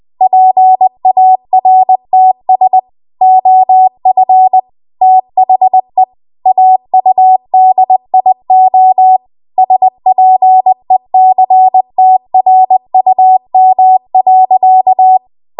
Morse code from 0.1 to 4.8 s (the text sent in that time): PARTS OF